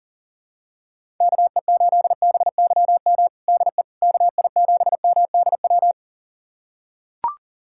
Morse code, heard as a keyed tone, about 40 wpm, 700 Hz. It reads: KE8BYM DE KI7MDW
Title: USA Callsign DE Callsign - 6 Characters - 40wpm